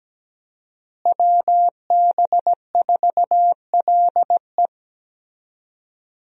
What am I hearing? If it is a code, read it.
Morse code, heard as a keyed tone, about 17 wpm, 700 Hz. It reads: WB4LE
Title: USA Call Signs - 5 Characters - Repeated 2x - 17wpm